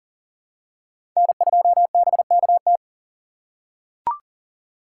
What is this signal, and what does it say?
Morse code, heard as a keyed tone, about 40 wpm, 700 Hz. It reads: N2BKT
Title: USA Call Signs - 5 Characters - 4x Character Spacing - 40wpm